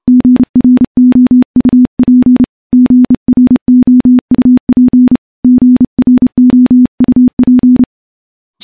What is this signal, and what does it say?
Morse code, heard as a keyed tone, about 30 wpm, 250 Hz. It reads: GROUP GROUP GROUP
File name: morse / morse-training-30WPM